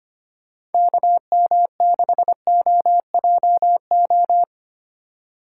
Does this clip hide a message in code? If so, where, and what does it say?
Yes: Morse code from 0.7 to 4.4 s, heard as a keyed tone, about 32 wpm, 700 Hz. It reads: KM6OJO